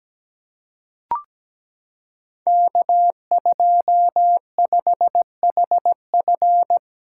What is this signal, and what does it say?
Morse code, heard as a keyed tone, about 17 wpm, 700 Hz. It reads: K25HF
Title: Code Groups - 5 Letter:Numbers - 17wpm